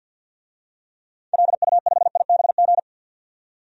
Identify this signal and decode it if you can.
Morse code, heard as a keyed tone, about 52 wpm, 700 Hz. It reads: LU5IBG
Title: DXCC-100 Call Signs - 6 Characters - 50wpm